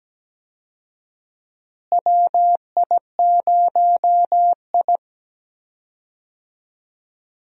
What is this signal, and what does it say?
Morse code, heard as a keyed tone, about 17 wpm, 700 Hz. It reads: WI0I
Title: USA Call Signs - 4 Characters - Repeated 2x - 17wpm